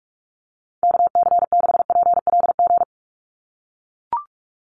Morse code, heard as a keyed tone, about 44 wpm, 700 Hz. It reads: KC6PLG
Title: USA Call Signs 45wpm